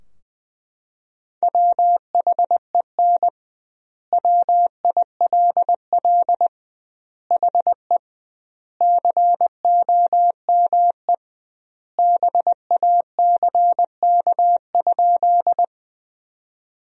Morse code, heard as a keyed tone, about 19 wpm, 700 Hz. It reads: WHEN WILL HE COME BACK?
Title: 2x Word-Spacing - Sentences from Top 200 Words - 20wpm